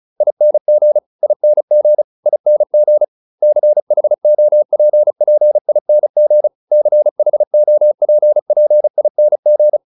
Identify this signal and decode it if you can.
Morse code, heard as a keyed tone, about 34 wpm, 600 Hz. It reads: ING ING ING CHOPPING CHOPPING